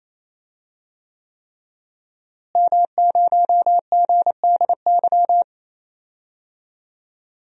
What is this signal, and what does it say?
Morse code, heard as a keyed tone, about 28 wpm, 700 Hz. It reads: M0GDY
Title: DXCC-100 Call Signs - 5 Characters - Repeated 3x - 28wpm